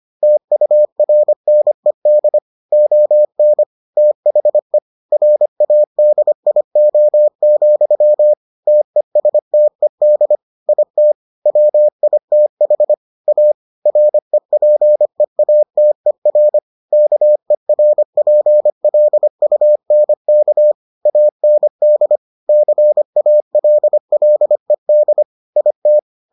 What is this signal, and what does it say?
Morse code, heard as a keyed tone, about 25 wpm, 600 Hz. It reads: TURNED ON THE RADIO, TESTED IT WITH A REPEATER KERPLUNK AND CALLED IT